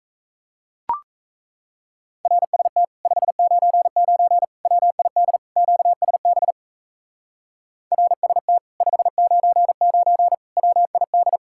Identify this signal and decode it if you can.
Morse code, heard as a keyed tone, about 38 wpm, 700 Hz. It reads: RST 599 WID QSB RST 599 WID